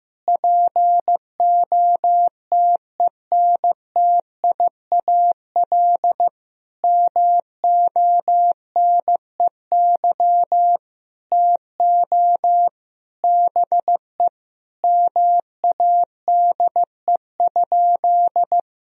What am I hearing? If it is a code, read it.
Morse code, heard as a keyed tone, about 15 wpm, 700 Hz. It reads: POTENTIAL MONEY TO BE MADE?
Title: Sentences from Top 1000 Words - Review Part 4 - 15wpm